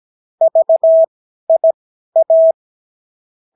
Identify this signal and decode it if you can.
Morse code, heard as a keyed tone, about 17 wpm, spaced as for 12 wpm, 650 Hz. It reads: VIA
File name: CW QSO libroCW4 18-8ppm